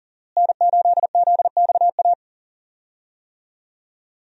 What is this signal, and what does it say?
Morse code, heard as a keyed tone, about 40 wpm, 700 Hz. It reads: N8ZXA
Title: USA Call Signs - 5 Characters - Repeated 3x - 40wpm